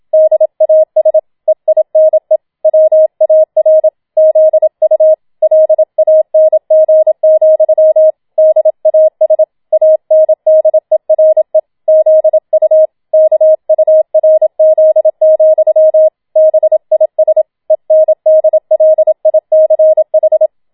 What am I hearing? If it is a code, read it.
Morse code, heard as a keyed tone, about 27 wpm, 630 Hz. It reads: DAS EINE WAR ZU LANG, DAS ANDERE ZU KURZ, BIS ENDLICH